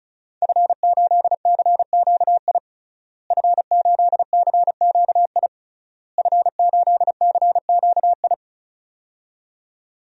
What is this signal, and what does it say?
Morse code, heard as a keyed tone, about 34 wpm, 700 Hz. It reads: F8CQI F8CQI F8CQI
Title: DXCC-100 Call Signs - 5 Characters - Repeated 3x - 35wpm